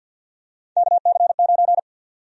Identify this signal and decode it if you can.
Morse code, heard as a keyed tone, about 48 wpm, 700 Hz. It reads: KC9
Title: USA Call Sign Prefixes - 50wpm